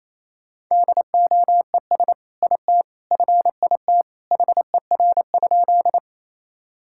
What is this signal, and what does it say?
Morse code, heard as a keyed tone, about 28 wpm, 700 Hz. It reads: DOES IT FIT HER?
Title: Sentences from Top 1000 Words - Review Part 5 - 28wpm